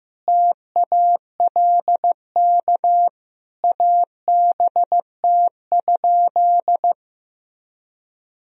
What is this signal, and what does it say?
Morse code, heard as a keyed tone, about 15 wpm, 700 Hz. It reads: TALK ABT?
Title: Sentences from Top 400 Words and Abbreviations 15wpm